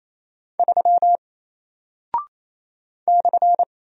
Morse code, heard as a keyed tone, about 27 wpm, 700 Hz. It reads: 3 /
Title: Single Character (Letter:Number:Punctuation:Prosign) Alt 28wpm